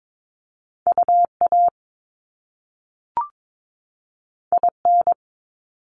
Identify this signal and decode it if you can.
Morse code, heard as a keyed tone, about 22 wpm, 700 Hz. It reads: UA IN
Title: Top Two Letter Combinations in Words 22wpm